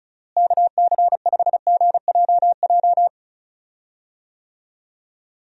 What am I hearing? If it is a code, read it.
Morse code, heard as a keyed tone, about 36 wpm, 700 Hz. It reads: KC5GJJ
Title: USA Call Signs - 6 Characters - Repeated 2x - 35wpm